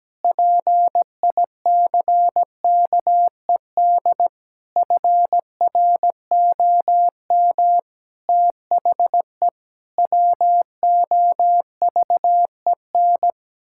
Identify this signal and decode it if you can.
Morse code, heard as a keyed tone, about 17 wpm, 700 Hz. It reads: PICKED FROM THE WOVEN
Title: Pangram Challenge v2 - 17wpm